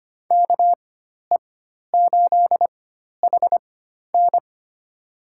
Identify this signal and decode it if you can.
Morse code, heard as a keyed tone, about 25 wpm, spaced as for 12 wpm, 700 Hz. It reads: KE8HN
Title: USA Call Signs - 4x Character Spacing - 25wpm